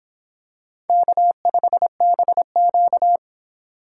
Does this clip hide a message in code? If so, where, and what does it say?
Yes: Morse code from 0.9 to 3.2 s, heard as a keyed tone, about 26 wpm, 700 Hz. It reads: K5BQ